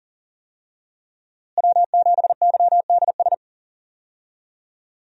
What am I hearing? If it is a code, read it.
Morse code, heard as a keyed tone, about 37 wpm, 700 Hz. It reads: W7YDS
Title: USA Call Signs - 5 Characters - Repeated 3x - 40wpm